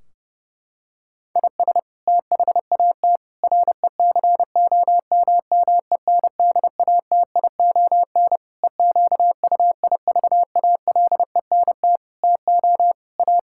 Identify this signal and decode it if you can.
Morse code, heard as a keyed tone, about 30 wpm, 700 Hz. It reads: IS THAT RECOMMENDATION EQUIVALENT TO A
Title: Sentences from Top 1600 Words 30wpm